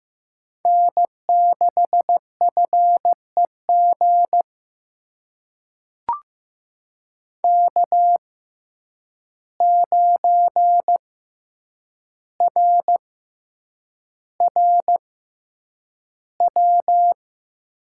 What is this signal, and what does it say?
Morse code, heard as a keyed tone, about 15 wpm, spaced as for 8 wpm, 700 Hz. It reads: N6FEG K9RRW
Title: USA Call Signs - 5 Characters - 6x Character Spacing - 15wpm